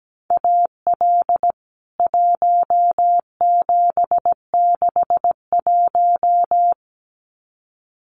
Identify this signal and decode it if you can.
Morse code, heard as a keyed tone, about 17 wpm, 700 Hz. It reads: AL 1761
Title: CWT Exchange 17wpm